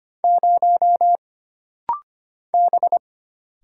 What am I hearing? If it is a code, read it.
Morse code, heard as a keyed tone, about 25 wpm, 700 Hz. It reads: 0 B
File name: Single Character (Letter:Number:Punctuation:Prosign) - Rapid-Fire - 25wpm